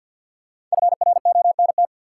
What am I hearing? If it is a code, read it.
Morse code, heard as a keyed tone, about 51 wpm, 700 Hz. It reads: FRONT
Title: Five Letter Words - 50wpm